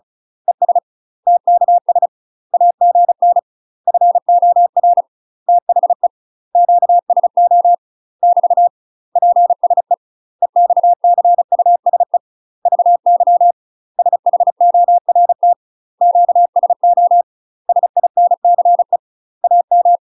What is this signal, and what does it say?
Morse code, heard as a keyed tone, about 34 wpm, 700 Hz. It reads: ES TKS AGN FOR THE QSO <BT> PSE EXCUSE VY SHORT QSO SINCE AM
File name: UK MASTER QSOs TRAINING/35WPM